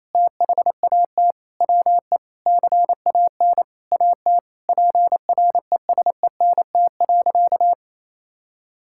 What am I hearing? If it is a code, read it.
Morse code, heard as a keyed tone, about 28 wpm, 700 Hz. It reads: THAT WE CAN AT PRESENT.
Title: Sentences from Top 1000 Words - Review Part 5 - 28wpm